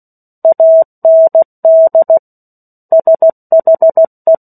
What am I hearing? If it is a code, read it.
Morse code, heard as a keyed tone, about 16 wpm, 650 Hz. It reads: AND SHE
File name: 03-Chapter-03 16 WPM